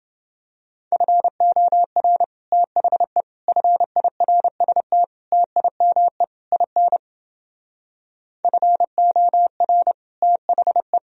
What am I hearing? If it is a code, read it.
Morse code, heard as a keyed tone, about 29 wpm, 700 Hz. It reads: FOR THE FIRST TIME IN FOR THE